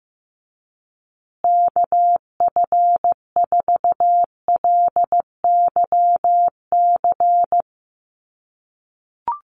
Morse code, heard as a keyed tone, about 15 wpm, 700 Hz. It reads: KF4LYC
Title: USA Call Signs Repeated 2x 15wpm